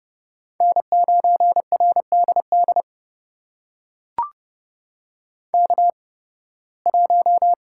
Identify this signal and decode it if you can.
Morse code, heard as a keyed tone, about 30 wpm, spaced as for 20 wpm, 700 Hz. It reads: N9RDD K1
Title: USA Call Signs - 5 Characters - 8x Character Spacing - 30wpm